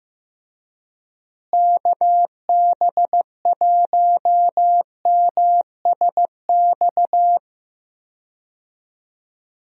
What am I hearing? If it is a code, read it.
Morse code, heard as a keyed tone, about 16 wpm, 700 Hz. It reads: KB1MSX
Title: USA Call Signs - 6 Characters - Repeated 3x - 15wpm